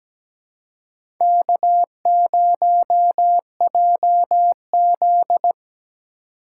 Morse code, heard as a keyed tone, about 17 wpm, 700 Hz. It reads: K0JZ
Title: USA Call Signs - 4x Character Spacing - 17wpm